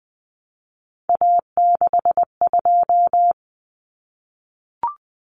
Morse code, heard as a keyed tone, about 20 wpm, 700 Hz. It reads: A62
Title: Common Cut Numbers - 3-Digit 20wpm